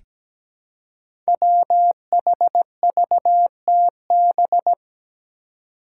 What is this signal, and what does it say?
Morse code, heard as a keyed tone, about 17 wpm, 700 Hz. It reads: WHVTB
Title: Five Challenging Letter-Number Combinations 17wpm